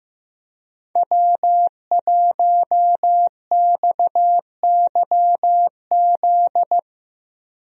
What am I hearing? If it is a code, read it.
Morse code, heard as a keyed tone, about 15 wpm, 700 Hz. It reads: W1XYZ
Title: USA Call Signs - 5 Characters - 15wpm